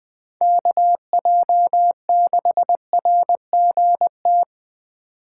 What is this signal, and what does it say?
Morse code, heard as a keyed tone, about 24 wpm, 700 Hz. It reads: KJ6RGT
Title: Speed Racing - USA 2x3 Call Signs 20wpm